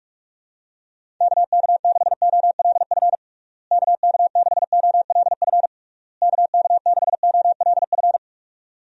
Morse code, heard as a keyed tone, about 41 wpm, 700 Hz. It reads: KK6OLF KK6OLF KK6OLF
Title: USA Call Signs - 6 Characters - Repeated 3x - 45wpm